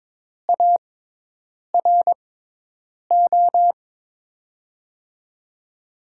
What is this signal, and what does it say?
Morse code, heard as a keyed tone, about 22 wpm, spaced as for 8 wpm, 700 Hz. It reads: ARO
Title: USA Call Signs - 6 Characters - 6x Character Spacing - 22wpm